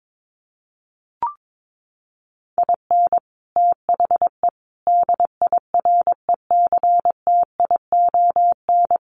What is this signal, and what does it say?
Morse code, heard as a keyed tone, about 22 wpm, 700 Hz. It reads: IN THE DIRECTION
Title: Sets of 5 Words 22wpm